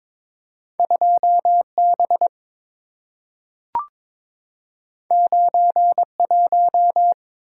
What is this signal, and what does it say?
Morse code, heard as a keyed tone, about 22 wpm, 690 Hz. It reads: 2B 91
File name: practice / 22wpm / Two Letter-Number Combinations - 22wpm